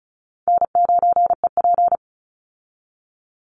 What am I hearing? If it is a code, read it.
Morse code, heard as a keyed tone, about 34 wpm, 700 Hz. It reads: N9EP
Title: Easy Call Signs Repeated 3x - 35wpm